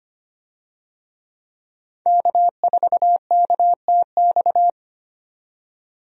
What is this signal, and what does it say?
Morse code, heard as a keyed tone, about 25 wpm, 700 Hz. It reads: K4KTX
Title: USA Call Signs - 5 Characters - 25wpm